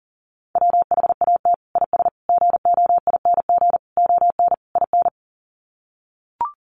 Morse code, heard as a keyed tone, about 40 wpm, 700 Hz. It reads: WHAT IS GOING ON IN
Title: Sets of 5 Words Encore2 40wpm